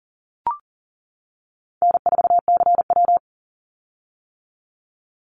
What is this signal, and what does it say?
Morse code, heard as a keyed tone, about 40 wpm, 700 Hz. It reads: N4CW
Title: Call Signs - 40wpm